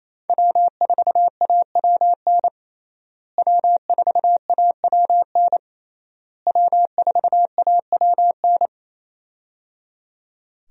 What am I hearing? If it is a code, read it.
Morse code, heard as a keyed tone, about 27 wpm, 700 Hz. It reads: W4AWN W4AWN W4AWN
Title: USA Call Signs - 5 Characters - Repeated 3x - 28wpm